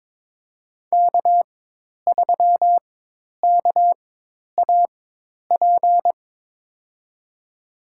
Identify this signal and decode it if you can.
Morse code, heard as a keyed tone, about 22 wpm, spaced as for 10 wpm, 700 Hz. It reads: K3KAP
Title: USA Call Signs - 5 Characters - 4x Character Spacing - 22wpm